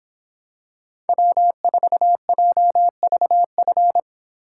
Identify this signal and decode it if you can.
Morse code, heard as a keyed tone, about 26 wpm, 700 Hz. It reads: W4JVF